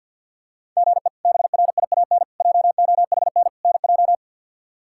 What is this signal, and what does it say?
Morse code, heard as a keyed tone, about 51 wpm, 700 Hz. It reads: GE BRIAN JOHN NJ
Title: K1USN SST QSO Practice - 50wpm